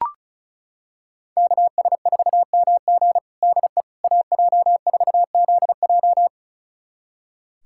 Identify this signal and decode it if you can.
Morse code, heard as a keyed tone, about 36 wpm, 700 Hz. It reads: KS4MG DE AJ4ZJ
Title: USA Callsign DE Callsign - 5 Characters - 35wpm